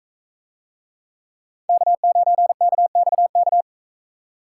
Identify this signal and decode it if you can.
Morse code, heard as a keyed tone, about 42 wpm, 700 Hz. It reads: K9KXK